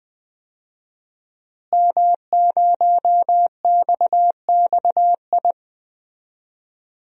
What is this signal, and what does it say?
Morse code, heard as a keyed tone, about 20 wpm, 700 Hz. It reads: M0XXI
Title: DXCC-100 Call Signs - 5 Characters - Repeated 3x - 20wpm